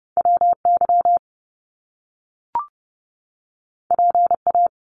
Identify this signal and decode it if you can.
Morse code, heard as a keyed tone, about 30 wpm, 700 Hz. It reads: WY PA
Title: US States 30wpm